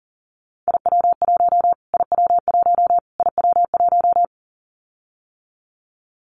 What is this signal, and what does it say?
Morse code, heard as a keyed tone, about 40 wpm, 700 Hz. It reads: IW1 IW1 IW1
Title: Callsign Prefixes for Top 25 DXCC - Repeated 3x - 40wpm